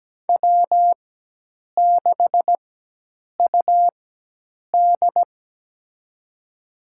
Morse code, heard as a keyed tone, about 17 wpm, spaced as for 8 wpm, 700 Hz. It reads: W6UD
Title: USA Call Signs - 4 Characters - 4x Character Spacing - 17wpm